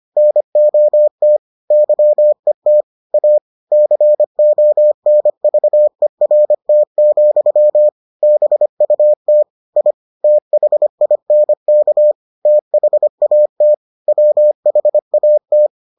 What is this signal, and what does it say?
Morse code, heard as a keyed tone, about 25 wpm, 600 Hz. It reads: NOT YET A CONVERT, BUT I THINK THAT WHAT